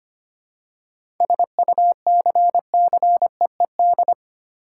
Morse code, heard as a keyed tone, about 25 wpm, 700 Hz. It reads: SUCCEED
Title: Seven Letter Words - 25wpm